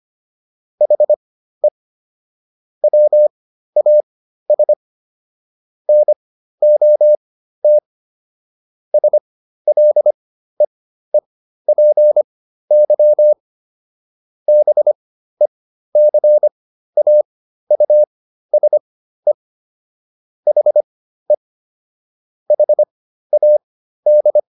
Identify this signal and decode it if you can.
Morse code, heard as a keyed tone, about 25 wpm, spaced as for 13 wpm, 600 Hz. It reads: HE WAS NOT SLEEPY BECAUSE HE HAD